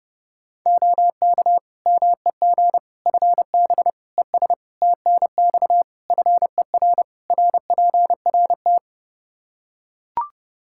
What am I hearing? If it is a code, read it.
Morse code, heard as a keyed tone, about 30 wpm, 700 Hz. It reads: OK MEG FB ES TNX FER RPRT
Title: Practice QSO - Exchange 2 - Info - 30wpm